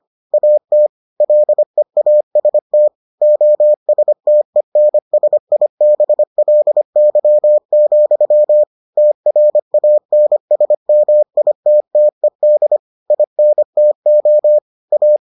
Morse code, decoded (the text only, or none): AT LEAST OSTENSIBLY, TRANSMITTED INTO A